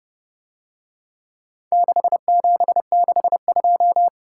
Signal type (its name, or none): Morse code